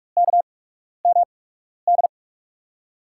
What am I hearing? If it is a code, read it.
Morse code, heard as a keyed tone, about 46 wpm, spaced as for 12 wpm, 700 Hz. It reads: KMD
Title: USA Call Signs - 6 Characters - 8x Character Spacing - 45wpm